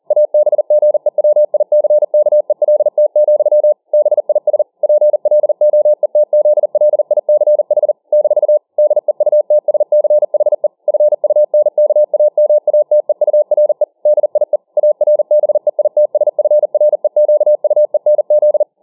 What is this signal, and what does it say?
Morse code, heard as a keyed tone, about 40 wpm, 600 Hz. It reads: ABGEWICKELT, BIS PLOETZLICH - DEUTSCHE FUNKAMATEURE DIE ARBEITSFREQUENZ